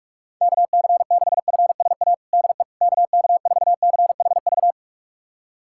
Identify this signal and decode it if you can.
Morse code, heard as a keyed tone, about 46 wpm, 690 Hz. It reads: KC6FSA DE KK4CHV
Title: USA Callsign DE Callsign 45wpm